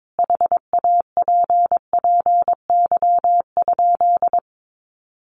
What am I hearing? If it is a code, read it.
Morse code, heard as a keyed tone, about 22 wpm, 700 Hz. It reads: HAPPY?
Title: Sentences from Top 1000 Words 22wpm